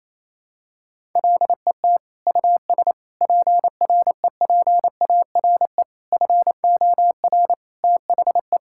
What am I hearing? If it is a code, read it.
Morse code, heard as a keyed tone, about 27 wpm, 700 Hz. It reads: LET US PREPARE FOR THE
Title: Sentences from Top 1000 Words - Review Part 6 - 28wpm